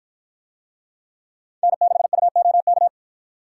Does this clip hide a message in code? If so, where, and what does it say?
Yes: Morse code from 1.6 to 2.9 s, heard as a keyed tone, about 53 wpm, 700 Hz. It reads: N6UOK